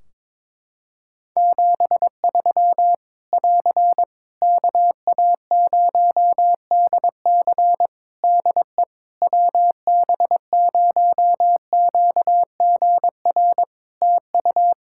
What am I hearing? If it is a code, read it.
Morse code, heard as a keyed tone, about 22 wpm, 700 Hz. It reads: 73 <AR> KA0DC DE WB0QGR TU
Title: Practice QSO - Exchange 4 - Ending - 22wpm